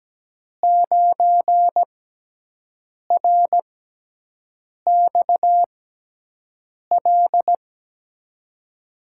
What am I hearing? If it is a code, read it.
Morse code, heard as a keyed tone, about 17 wpm, spaced as for 6 wpm, 700 Hz. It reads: 9RXL
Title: USA Call Signs - 6 Characters - 6x Character Spacing - 17wpm